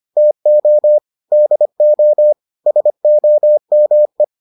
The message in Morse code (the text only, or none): TO DO SOME